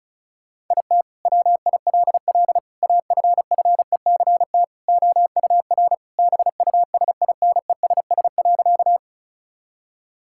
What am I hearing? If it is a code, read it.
Morse code, heard as a keyed tone, about 36 wpm, 700 Hz. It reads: IT WILL AFFECT OUR BUSINESS.
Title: Sentences from Top 1000 Words - Review Part 6 - 35wpm